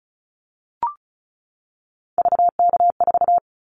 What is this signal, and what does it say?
Morse code, heard as a keyed tone, about 36 wpm, 700 Hz. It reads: VK4
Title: Callsign Prefixes for Top 50 DXCC - 35wpm